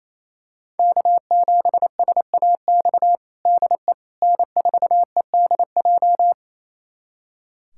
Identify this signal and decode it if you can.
Morse code, heard as a keyed tone, about 28 wpm, 700 Hz. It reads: K7SAX DE N4EDJ
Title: USA Callsign DE Callsign - 5 Characters - 28wpm